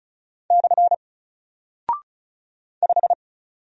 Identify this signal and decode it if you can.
Morse code, heard as a keyed tone, about 34 wpm, 700 Hz. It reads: / 5
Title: Single Character (Letter:Number:Punctuation:Prosign) Alt 35wpm